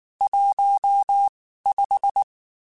Morse code, heard as a keyed tone, about 19 wpm, spaced as for 14 wpm, 800 Hz. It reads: 15
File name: (spaced) / QSOs Spaced-mp3s / WD0BGP 1 @18 sp